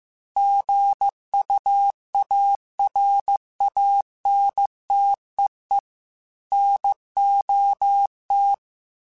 Morse code, heard as a keyed tone, about 15 wpm, 800 Hz. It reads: GUARANTEE NOT
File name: Chuckles / Chuckles-mp3s / Guaranteed @13